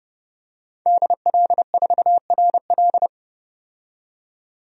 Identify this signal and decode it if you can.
Morse code, heard as a keyed tone, about 30 wpm, 700 Hz. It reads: DL4RL
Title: DXCC-100 Call Signs - 5 Characters - 30wpm